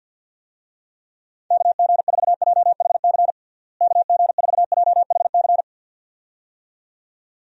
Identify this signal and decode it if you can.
Morse code, heard as a keyed tone, about 50 wpm, 700 Hz. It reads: KG4JHC KG4JHC
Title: USA Call Signs - 6 Characters - Repeated 2x - 50wpm